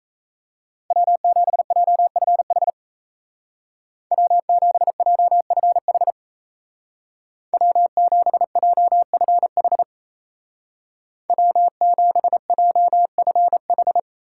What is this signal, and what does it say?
Morse code, heard as a keyed tone, about 33 wpm, 700 Hz. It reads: W7JFH W7JFH W7JFH W7JFH
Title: Speed Racing - USA 1x3 Call Signs 28wpm